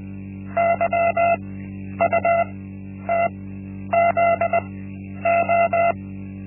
Morse code, heard as a keyed tone, about 20 wpm, spaced as for 10 wpm, 680 Hz. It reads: YUTZO
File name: Linksmakalnis-1